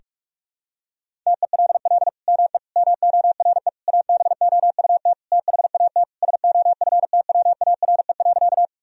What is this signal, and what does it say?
Morse code, heard as a keyed tone, about 47 wpm, 700 Hz. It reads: TELL ME MORE ABOUT THAT SOFTWARE.
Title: Sentences from Top 1000 Words - Review Part 3 - 45wpm